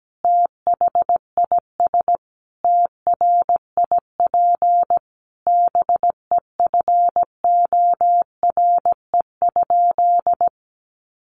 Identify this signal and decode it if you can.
Morse code, heard as a keyed tone, about 17 wpm, 700 Hz. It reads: THIS TRIP BEFORE?
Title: Sentences from Top 1300 Words 17wpm